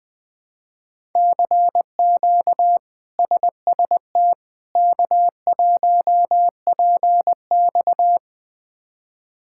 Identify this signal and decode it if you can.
Morse code, heard as a keyed tone, about 20 wpm, 700 Hz. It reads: CQ SST K1PX
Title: K1USN SST QSO Practice - 20wpm